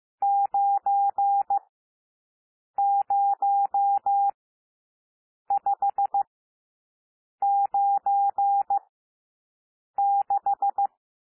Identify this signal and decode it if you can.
Morse code, heard as a keyed tone, about 15 wpm, spaced as for 6 wpm, 800 Hz. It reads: 90596
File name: SOUND NUMBERS 15 wpm